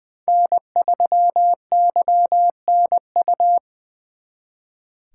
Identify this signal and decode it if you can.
Morse code, heard as a keyed tone, about 24 wpm, 700 Hz. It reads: N3YNU
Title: Speed Racing - USA 1x3 Call Signs 20wpm